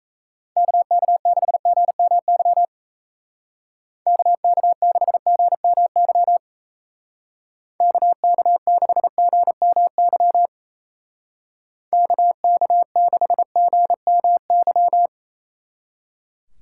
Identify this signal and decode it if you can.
Morse code, heard as a keyed tone, about 34 wpm, 700 Hz. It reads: KK6GMY KK6GMY KK6GMY KK6GMY
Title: Speed Racing - USA 2x3 Call Signs 28wpm